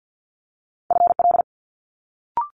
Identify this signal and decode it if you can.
Morse code, heard as a keyed tone, about 50 wpm, 700 Hz. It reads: FL
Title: Multi-letter Phonemes Quick Fire 50wpm